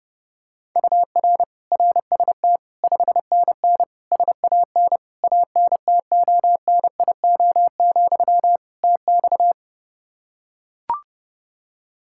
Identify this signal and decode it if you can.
Morse code, heard as a keyed tone, about 29 wpm, 700 Hz. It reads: UR RST 5NN SAN ANTONIO, TX
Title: Practice QSO - Exchange 1 - Beginning - 30wpm